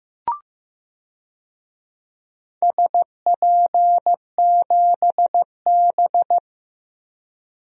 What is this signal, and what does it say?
Morse code, heard as a keyed tone, about 15 wpm, 700 Hz. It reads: SP7B
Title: DXCC-100 Call Signs - 4 Characters - 15wpm